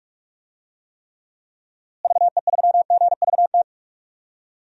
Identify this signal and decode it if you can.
Morse code, heard as a keyed tone, about 42 wpm, 700 Hz. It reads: VE3GVT
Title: DXCC-100 Call Signs - 6 Characters - Repeated 3x - 45wpm